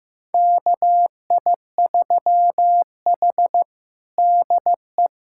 Morse code, heard as a keyed tone, about 15 wpm, 700 Hz. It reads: KI3H DE
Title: USA Callsign DE Callsign - 4 Characters - 15wpm